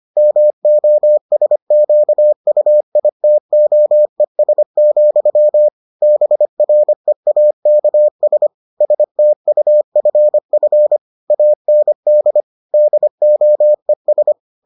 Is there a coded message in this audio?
Yes: Morse code from 0.2 to 14.3 s, heard as a keyed tone, about 25 wpm, 600 Hz. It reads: MOSQUITOES, BREAKS STUFF AND DOES